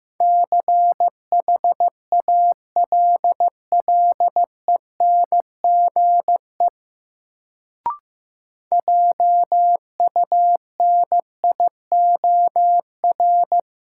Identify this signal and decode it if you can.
Morse code, heard as a keyed tone, about 15 wpm, 700 Hz. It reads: CHALLENGE JUNIOR
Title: Sentences from Top 1000 Words - Review Part 7 - New Word Drill - 15wpm